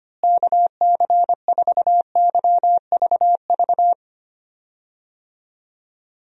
Morse code, heard as a keyed tone, about 25 wpm, 700 Hz. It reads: KC4YVV
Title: USA Call Signs - 6 Characters - Repeated 3x - 25wpm